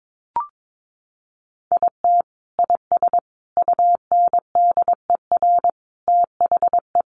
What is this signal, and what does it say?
Morse code, heard as a keyed tone, about 22 wpm, 700 Hz. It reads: IT IS UNDER THE
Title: Sentences from Top 1400 Words 22wpm